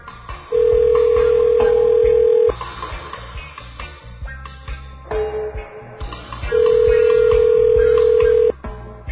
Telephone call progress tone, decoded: ringback tone